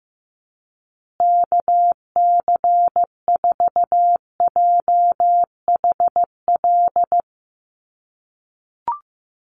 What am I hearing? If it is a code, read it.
Morse code, heard as a keyed tone, about 15 wpm, 700 Hz. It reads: KC4JHL
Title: USA Call Signs Repeated 2x 15wpm